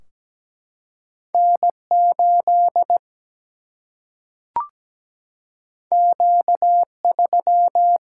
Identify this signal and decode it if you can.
Morse code, heard as a keyed tone, about 17 wpm, 700 Hz. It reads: N8 Q3
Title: Single-Letter Single-Number 17wpm